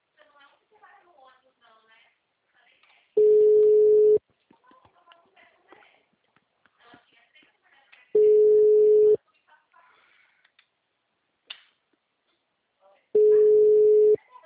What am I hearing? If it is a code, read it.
Ringback tone (Europe, 425 Hz), with speech.